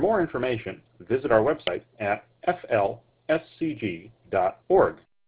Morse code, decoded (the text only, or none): none